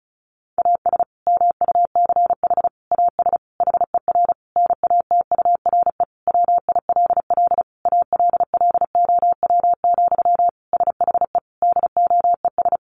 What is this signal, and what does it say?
Morse code, heard as a keyed tone, about 34 wpm, 700 Hz. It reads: AS MUCH AS HER NATURE WILL ALLOW, SHE DOES
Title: Sentences from Top 700 Words 35wpm